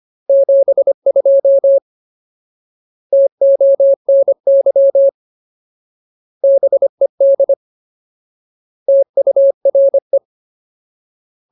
Morse code, heard as a keyed tone, about 25 wpm, 550 Hz. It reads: 72 TONY BED TURE
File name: Ultimate Word List / 25WPM Ultimate